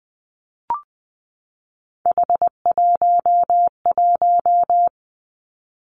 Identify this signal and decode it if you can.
Morse code, heard as a keyed tone, about 20 wpm, 700 Hz. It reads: H11
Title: Three Challenging Letter-Number Combinations 20wpm